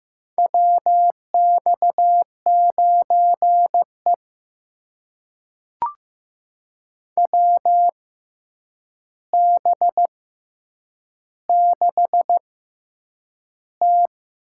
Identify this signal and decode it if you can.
Morse code, heard as a keyed tone, about 15 wpm, spaced as for 8 wpm, 700 Hz. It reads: WX9E WB6T
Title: USA Call Signs - 4 Characters - 6x Character Spacing - 15wpm